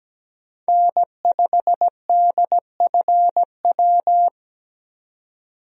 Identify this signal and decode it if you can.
Morse code, heard as a keyed tone, about 17 wpm, 700 Hz. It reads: N5DFW
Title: USA Call Signs - 5 Characters - 17wpm